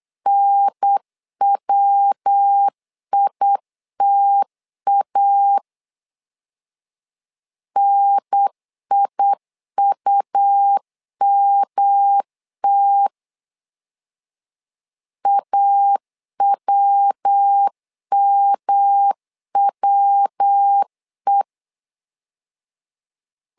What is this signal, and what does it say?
Morse code, heard as a keyed tone, about 8 wpm, 800 Hz. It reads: NWITA NIUMT AWMWE
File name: LEZIONI / 3 MWRO / 3 MW 8wpm 2501